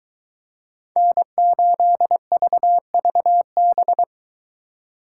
Morse code, heard as a keyed tone, about 23 wpm, 700 Hz. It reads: N8VVB